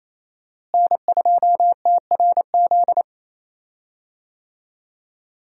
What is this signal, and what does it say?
Morse code, heard as a keyed tone, about 28 wpm, 700 Hz. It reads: N2TRZ
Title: USA Call Signs - 5 Characters - Repeated 2x - 28wpm